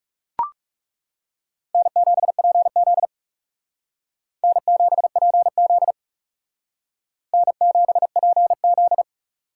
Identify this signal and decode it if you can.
Morse code, heard as a keyed tone, about 37 wpm, 700 Hz. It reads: N7PZ N7PZ N7PZ
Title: Speed Racing - USA 1x2 Call Signs 30wpm